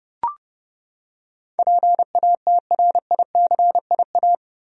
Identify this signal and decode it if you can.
Morse code, heard as a keyed tone, about 32 wpm, 700 Hz. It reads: PATRICIA
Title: USA Common Names - 30wpm